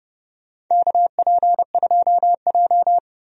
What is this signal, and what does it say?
Morse code, heard as a keyed tone, about 30 wpm, 700 Hz. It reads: KP2J
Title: DXCC-100 Call Signs - 4 Characters - 30wpm